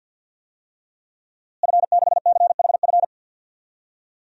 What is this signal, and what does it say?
Morse code, heard as a keyed tone, about 52 wpm, 700 Hz. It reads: F6CHF
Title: DXCC-100 Call Signs - 5 Characters - Repeated 3x - 50wpm